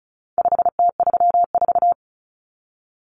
Morse code, heard as a keyed tone, about 34 wpm, 700 Hz. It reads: 5T34
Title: Common Cut Numbers - 4-Digit 35wpm